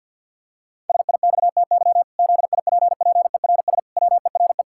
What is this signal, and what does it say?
Morse code, heard as a keyed tone, about 51 wpm, 700 Hz. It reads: SIXTY ZIPPERS WERE
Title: Pangram Challenge v2 - 50wpm